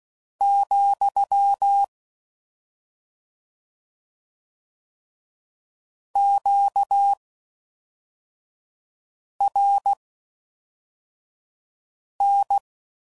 Morse code, heard as a keyed tone, about 16 wpm, spaced as for 4 wpm, 800 Hz. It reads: , QRN